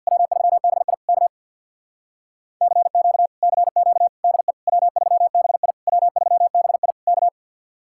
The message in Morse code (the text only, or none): W3BI K CQ CQ DE W3BI W3BI K